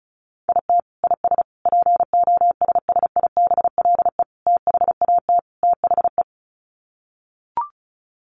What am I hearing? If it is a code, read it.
Morse code, heard as a keyed tone, about 34 wpm, 700 Hz. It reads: IT IS POSSIBLE THAT THE
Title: Sets of 5 Words Encore2 35wpm